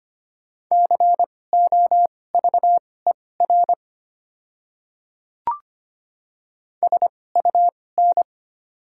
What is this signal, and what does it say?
Morse code, heard as a keyed tone, about 25 wpm, spaced as for 18 wpm, 700 Hz. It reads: COVER SUN
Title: 2x Character Spacing - Top 300 Words - 25wpm